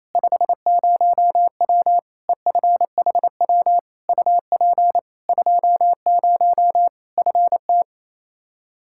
Morse code, heard as a keyed tone, about 28 wpm, 700 Hz. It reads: 50W EFHW UP 20 FT
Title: Practice QSO - Exchange 2 - Info - 28wpm